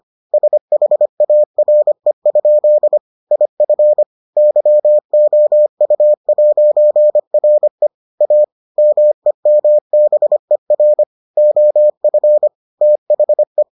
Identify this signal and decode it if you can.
Morse code, heard as a keyed tone, about 25 wpm, 600 Hz. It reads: SHARE? IF YOU'RE A MEMBER OF THE